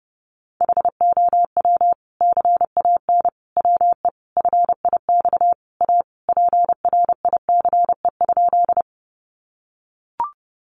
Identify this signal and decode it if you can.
Morse code, heard as a keyed tone, about 30 wpm, 700 Hz. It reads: HOW CAN WE FIX A PRICE?
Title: Sentences from Top 1400 Words 30wpm